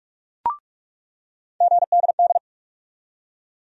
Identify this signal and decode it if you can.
Morse code, heard as a keyed tone, about 46 wpm, 700 Hz. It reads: GDD
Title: USA Call Sign Postfixes - 3 Letters 45wpm